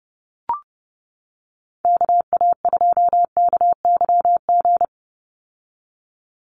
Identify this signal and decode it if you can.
Morse code, heard as a keyed tone, about 30 wpm, 700 Hz. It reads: KA2KYG
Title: USA Call Signs 30wpm